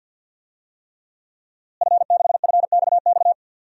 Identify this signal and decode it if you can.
Morse code, heard as a keyed tone, about 48 wpm, 700 Hz. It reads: F6FXX
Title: DXCC-100 Call Signs - 5 Characters - Repeated 3x - 50wpm